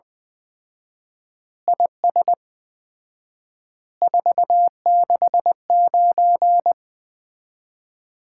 Morse code, heard as a keyed tone, about 20 wpm, 700 Hz. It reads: IS 469
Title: Common QSO Phrases - 4x Word-Spacing - 20wpm